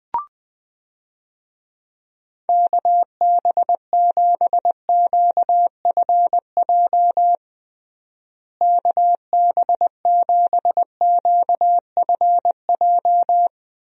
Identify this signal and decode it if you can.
Morse code, heard as a keyed tone, about 20 wpm, 700 Hz. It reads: KB7QFJ KB7QFJ
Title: USA Call Signs - 6 Characters - Repeated 2x - 20wpm